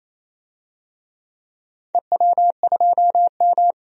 Morse code, heard as a keyed tone, about 28 wpm, 700 Hz. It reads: EW2M
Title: DXCC-100 Call Signs - 4 Characters - 28wpm